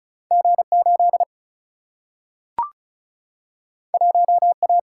Morse code, heard as a keyed tone, about 36 wpm, 700 Hz. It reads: G8 1A
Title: Single-Letter Single-Number 35wpm